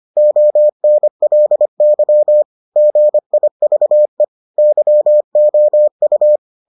Morse code, heard as a keyed tone, about 25 wpm, 600 Hz. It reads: ONLY GIVE YOU